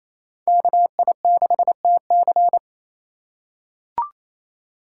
Morse code, heard as a keyed tone, about 27 wpm, 700 Hz. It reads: KI6TC
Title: USA Call Signs - 5 Characters - 6x Character Spacing - 28wpm